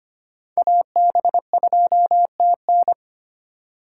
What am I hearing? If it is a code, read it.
Morse code, heard as a keyed tone, about 25 wpm, 700 Hz. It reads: AB2TN
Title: USA Call Signs - 5 Characters - 2x Character Spacing - 25wpm